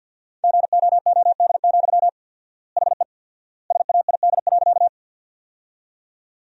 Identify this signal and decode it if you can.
Morse code, heard as a keyed tone, about 50 wpm, 700 Hz. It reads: GOOD, HE SAID.
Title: 4x Word-Spacing - Sentences from Top 200 Words - 50wpm